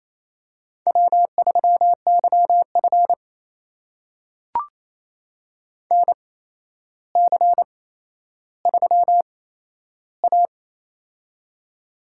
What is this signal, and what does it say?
Morse code, heard as a keyed tone, about 28 wpm, spaced as for 12 wpm, 700 Hz. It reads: W3YF NC3A
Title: USA Call Signs - 4 Characters - 8x Character Spacing - 28wpm